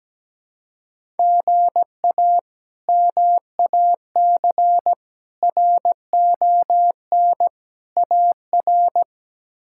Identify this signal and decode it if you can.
Morse code, heard as a keyed tone, about 17 wpm, 700 Hz. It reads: GA MAC RON AR
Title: K1USN SST QSO Practice - 17wpm